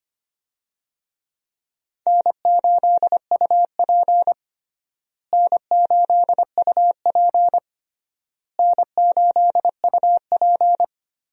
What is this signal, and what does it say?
Morse code, heard as a keyed tone, about 25 wpm, 700 Hz. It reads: N8UP N8UP N8UP
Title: USA Call Signs - 4 Characters - Repeated 3x - 25wpm